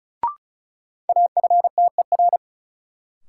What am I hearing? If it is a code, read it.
Morse code, heard as a keyed tone, about 36 wpm, 700 Hz. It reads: AFTER
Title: Sentences from Top 200 Words - New Word Drill - 35wpm